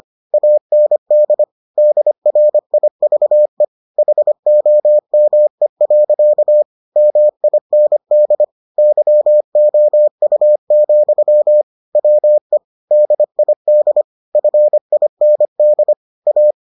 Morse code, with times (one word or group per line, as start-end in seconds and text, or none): 0.3-1.4 AND
1.8-3.6 DRIVE
4.0-6.6 HOME.
7.0-8.4 MIND
8.8-11.6 YOU,
12.0-12.6 WE
12.9-14.0 DID
14.3-15.9 FIND
16.3-16.5 A